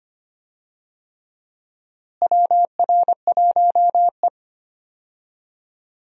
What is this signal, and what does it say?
Morse code, heard as a keyed tone, about 25 wpm, 700 Hz. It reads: WR1E
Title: USA Call Signs - 4 Characters - Repeated 3x - 25wpm